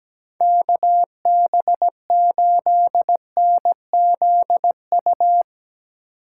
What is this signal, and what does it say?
Morse code, heard as a keyed tone, about 17 wpm, 700 Hz. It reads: KB8NZU
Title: USA Call Signs - 6 Characters - 8x Character Spacing - 17wpm